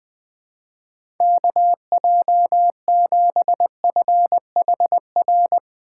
Morse code, heard as a keyed tone, about 20 wpm, 700 Hz. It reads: KJ7FHR
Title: USA Call Signs - 6 Characters - 2x Character Spacing - 20wpm